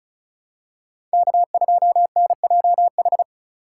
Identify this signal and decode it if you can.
Morse code, heard as a keyed tone, about 34 wpm, 700 Hz. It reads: K2NJH
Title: USA Call Signs - 5 Characters - 2x Character Spacing - 35wpm